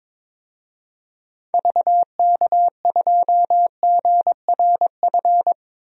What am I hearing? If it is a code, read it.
Morse code, heard as a keyed tone, about 22 wpm, 700 Hz. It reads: VK2GRF
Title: DXCC-100 Call Signs - 6 Characters - Repeated 3x - 22wpm